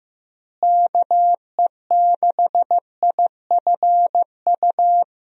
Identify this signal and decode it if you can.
Morse code, heard as a keyed tone, about 15 wpm, 700 Hz. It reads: KE6IFU
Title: USA Call Signs - 6 Characters - Repeated 2x - 15wpm